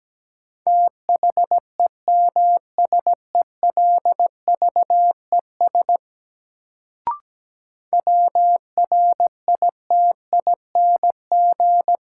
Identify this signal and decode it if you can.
Morse code, heard as a keyed tone, about 17 wpm, 700 Hz. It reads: THEMSELVES WRITING
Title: Sentences from Top 800 Words - New Word Drill - 17wpm